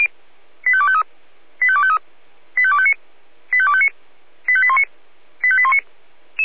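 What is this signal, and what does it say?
Five-tone selective call (ZVEI), heard as quick runs of five tones, tones of about 70 ms. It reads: 85253 85253 85259 85259 85819 85819
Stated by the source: Five-tone ZVEI selective call 85857, received on an SDR.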